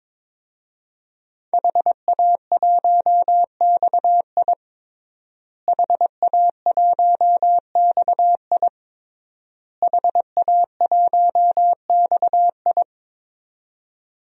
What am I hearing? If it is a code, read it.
Morse code, heard as a keyed tone, about 22 wpm, 700 Hz. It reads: HA1XI HA1XI HA1XI
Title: DXCC-100 Call Signs - 5 Characters - Repeated 3x - 22wpm